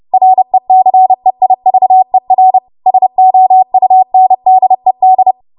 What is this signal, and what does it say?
Morse code, heard as a keyed tone, about 30 wpm, 750 Hz. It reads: RECEIVER SOUNDED
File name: Archive / 30 WPM / 30WPM